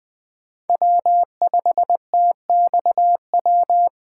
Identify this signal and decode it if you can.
Morse code, heard as a keyed tone, about 20 wpm, 700 Hz. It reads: W5TXW
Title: USA Call Signs - 3x Character Spacing - 20wpm